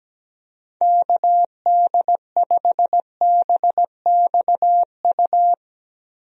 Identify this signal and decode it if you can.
Morse code, heard as a keyed tone, about 17 wpm, 700 Hz. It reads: KD5BXU
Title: USA Call Signs - 6 Characters - Repeated 3x - 17wpm